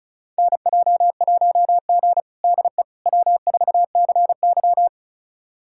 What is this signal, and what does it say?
Morse code, heard as a keyed tone, about 36 wpm, 700 Hz. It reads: NJ1G DE W4CY
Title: USA Callsign DE Callsign - 4 Characters - 35wpm